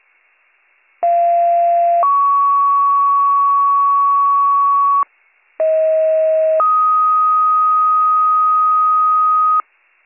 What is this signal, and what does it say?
Two-tone page, heard as a one-second tone, then a three-second tone, about 690 then 1120 Hz, about 640 then 1250 Hz. It is heard from 1.0 to 9.6 s.